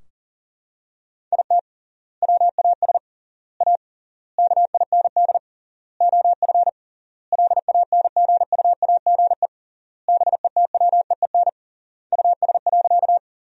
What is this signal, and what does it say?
Morse code, heard as a keyed tone, about 40 wpm, 700 Hz. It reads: IT WAS A KIND OF LANGUAGE BETWEEN US.
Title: 3x Word-Spacing - Sentences from Top 500 Words - 40wpm